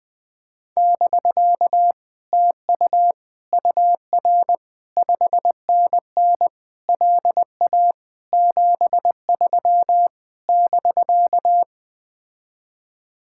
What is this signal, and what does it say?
Morse code, heard as a keyed tone, about 20 wpm, 700 Hz. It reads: <BK> TU UR 5NN LA 73 <BK>
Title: Speed-Racing Common QSO Phrases - 17wpm